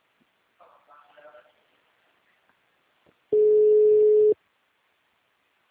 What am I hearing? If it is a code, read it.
Ringback tone (Europe, 425 Hz), with speech.
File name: out- -2003- - - .17904